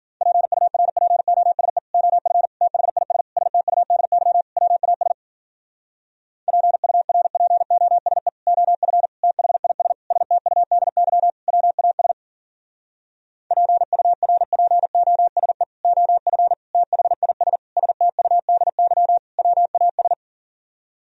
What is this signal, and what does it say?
Morse code, heard as a keyed tone, about 51 wpm, 700 Hz. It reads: PURPOSE OF THIS STUDY WAS PURPOSE OF THIS STUDY WAS PURPOSE OF THIS STUDY WAS